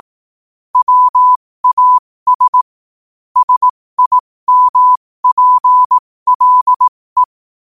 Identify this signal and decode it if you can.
Morse code, heard as a keyed tone, about 18 wpm, spaced as for 15 wpm, 1000 Hz. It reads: WAS SIMPLE
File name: qotdfiles / 15wpm